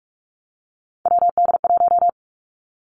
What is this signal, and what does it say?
Morse code, heard as a keyed tone, about 46 wpm, 700 Hz. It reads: WD1
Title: Three Challenging Letter-Number Combinations 45wpm